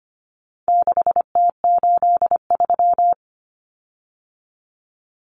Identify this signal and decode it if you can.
Morse code, heard as a keyed tone, about 25 wpm, 700 Hz. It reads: 6T83
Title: Common Cut Numbers - 4-Digit 25wpm